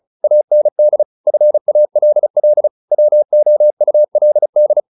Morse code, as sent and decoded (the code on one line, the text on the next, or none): .- -. -.. / ..-. .- .-.. .-.. / .-- --- ..- .-.. -..
AND FALL WOULD